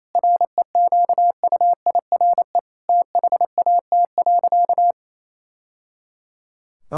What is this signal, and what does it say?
Morse code, heard as a keyed tone, about 28 wpm, 700 Hz. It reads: REQUIRE THAT.
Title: Sentences from Top 1000 Words - Review Part 6 - 28wpm